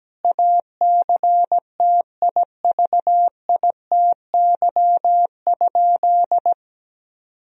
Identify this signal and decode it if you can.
Morse code, heard as a keyed tone, about 17 wpm, 700 Hz. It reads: ACTIVITY?
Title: Sentences from Top 1000 Words - Review Part 4 - 17wpm